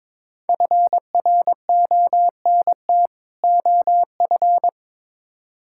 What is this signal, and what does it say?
Morse code, heard as a keyed tone, about 22 wpm, 700 Hz. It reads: FRONT OF
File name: Build A Phrase - 2-Word Phrases - 22wpm